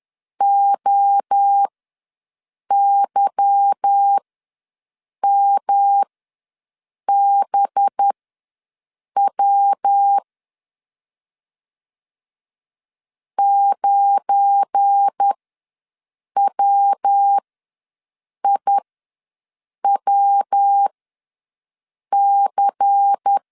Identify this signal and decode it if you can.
Morse code, heard as a keyed tone, about 10 wpm, spaced as for 6 wpm, 800 Hz. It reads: OYMBW 9WIWC
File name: LEZIONI / 8 YJ45 / 8 YJ ws 10wpm 4043